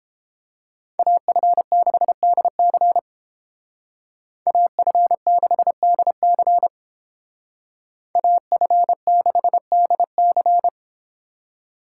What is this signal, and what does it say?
Morse code, heard as a keyed tone, about 30 wpm, 700 Hz. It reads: AF6DC AF6DC AF6DC